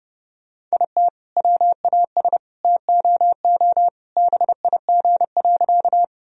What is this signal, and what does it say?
Morse code, heard as a keyed tone, about 30 wpm, 700 Hz. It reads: IT WAS TOO BIG.
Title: Sentences from Top 1000 Words - Review Part 1 - 30wpm